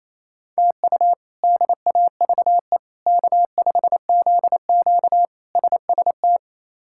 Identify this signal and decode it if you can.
Morse code, heard as a keyed tone, about 28 wpm, 700 Hz. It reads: TU DAVE K5ZQ SST
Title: K1USN SST QSO Practice - 28wpm